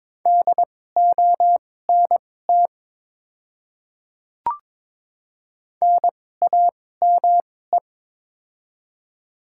Morse code, heard as a keyed tone, about 22 wpm, spaced as for 16 wpm, 700 Hz. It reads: DONT NAME
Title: 2x Character Spacing - Top 300 Words - 22wpm